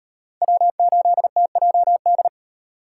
Morse code, heard as a keyed tone, about 35 wpm, 700 Hz. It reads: W8TJD